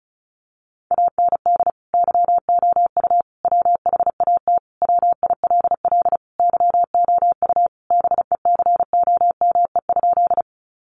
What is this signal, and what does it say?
Morse code, heard as a keyed tone, about 34 wpm, 700 Hz. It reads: AND YOU WHAT WILL YOU BECOME?
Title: Sentences from Top 600 Words 35wpm